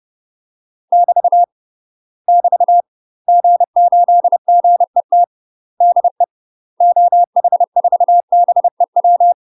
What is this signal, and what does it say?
Morse code, heard as a keyed tone, about 30 wpm, 700 Hz. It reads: <BT> <BT> G8GET DE OH4BEW
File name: UK MASTER QSOs TRAINING / 30WPM / UK MASTER QSOs 0016